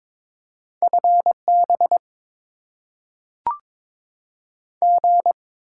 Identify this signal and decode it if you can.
Morse code, heard as a keyed tone, about 22 wpm, 700 Hz. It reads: F B G
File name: Top 20 QSO Elements - 6x Character Spacing - 22wpm